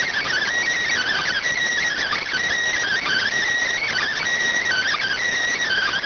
Emergency Alert System (SAME) tones (header bursts), heard as harsh buzzing bursts. It is heard from 0.9 to 5.2 s.